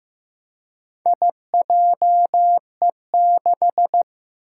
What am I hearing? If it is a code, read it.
Morse code, heard as a keyed tone, about 16 wpm, 700 Hz. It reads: IJE6
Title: Four Challenging Letter-Number Combinations 15wpm